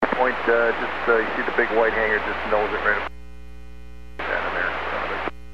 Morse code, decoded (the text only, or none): none